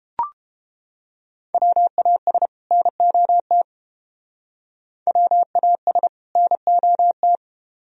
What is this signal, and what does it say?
Morse code, heard as a keyed tone, about 28 wpm, 700 Hz. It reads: WAS NOT WAS NOT